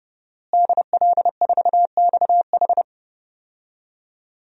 Morse code, heard as a keyed tone, about 30 wpm, 700 Hz. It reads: DL4XH
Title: DXCC-100 Call Signs - 5 Characters - Repeated 3x - 30wpm